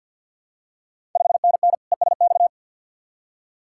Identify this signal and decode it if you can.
Morse code, heard as a keyed tone, about 53 wpm, 690 Hz. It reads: 5NN ESX
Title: Ontario QSO Party Exchange 50wpm